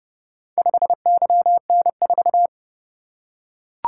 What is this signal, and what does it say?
Morse code, heard as a keyed tone, about 30 wpm, 700 Hz. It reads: 5YN4